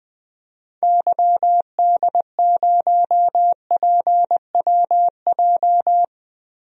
Morse code, heard as a keyed tone, about 20 wpm, 700 Hz. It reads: YD0PWJ
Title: DXCC-100 Call Signs - 6 Characters - 20wpm